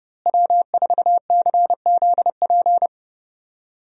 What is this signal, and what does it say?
Morse code, heard as a keyed tone, about 30 wpm, 700 Hz. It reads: W4CZP
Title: USA Call Signs - 5 Characters - Repeated 3x - 30wpm